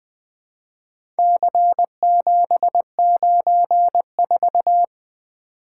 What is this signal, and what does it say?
Morse code, heard as a keyed tone, about 20 wpm, 700 Hz. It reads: C794
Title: Four Letter-Number Combinations - 20wpm